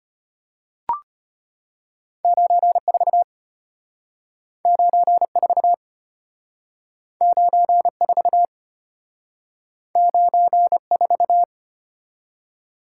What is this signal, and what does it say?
Morse code, heard as a keyed tone, about 32 wpm, 700 Hz. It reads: 94 94 94 94